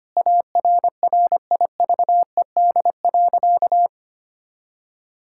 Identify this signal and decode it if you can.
Morse code, heard as a keyed tone, about 25 wpm, 700 Hz. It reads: ARRIVED.
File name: practice / 25wpm / Sentences with 2-Word Phrases - 25wpm